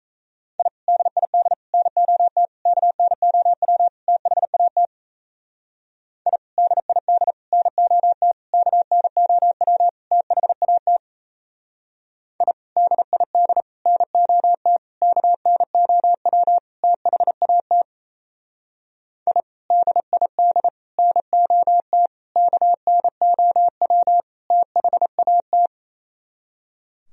Morse code, heard as a keyed tone, about 33 wpm, 700 Hz. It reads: I DID NOT KNOW THAT I DID NOT KNOW THAT I DID NOT KNOW THAT I DID NOT KNOW THAT
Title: Speed-Racing - Sets of 5 Words 28wpm